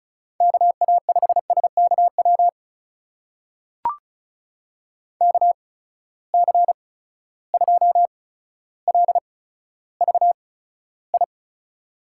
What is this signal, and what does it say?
Morse code, heard as a keyed tone, about 36 wpm, spaced as for 15 wpm, 700 Hz. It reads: KA5SKW KC2LVI
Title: USA Call Signs - 6 Characters - 8x Character Spacing - 35wpm